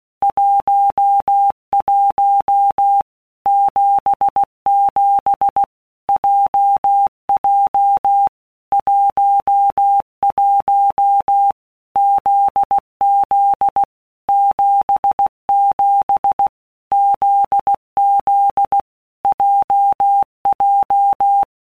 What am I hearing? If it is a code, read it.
Morse code, heard as a keyed tone, about 16 wpm, 800 Hz. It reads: 11 77 JJ 11 ZZ 77 ZZ JJ
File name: Basic Letter Groups By Blocks / Groups@15 / JZ17@15